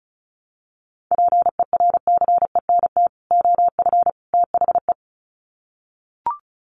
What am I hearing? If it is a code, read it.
Morse code, heard as a keyed tone, about 34 wpm, 700 Hz. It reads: PERCENT OF THE
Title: Sets of 3 Words 35wpm